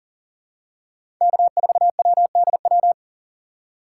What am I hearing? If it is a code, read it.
Morse code, heard as a keyed tone, about 40 wpm, 700 Hz. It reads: K4WDW
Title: USA Call Signs - 5 Characters - 6x Character Spacing - 40wpm